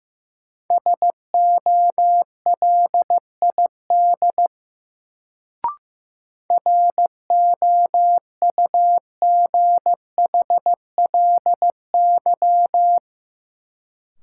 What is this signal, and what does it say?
Morse code, heard as a keyed tone, about 15 wpm, 700 Hz. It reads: SOLID ROUGHLY
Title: Sentences from Top 1500 Words - New Word Drill Alt - 15wpm